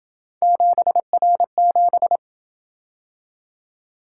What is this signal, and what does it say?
Morse code, heard as a keyed tone, about 27 wpm, 700 Hz. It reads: 7R7